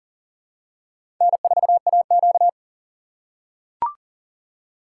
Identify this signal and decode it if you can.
Morse code, heard as a keyed tone, about 40 wpm, 700 Hz. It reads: N4AQ
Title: USA Call Signs - 4 Characters - 6x Character Spacing - 40wpm